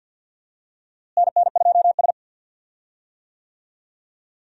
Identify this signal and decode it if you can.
Morse code, heard as a keyed tone, about 48 wpm, 700 Hz. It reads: NN2S
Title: USA Call Signs - 4 Characters - Repeated 3x - 50wpm